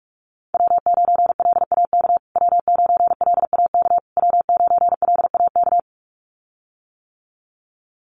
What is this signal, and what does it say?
Morse code, heard as a keyed tone, about 44 wpm, 700 Hz. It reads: W9LAK W9LAK W9LAK
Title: USA Call Signs Repeated 3x 45wpm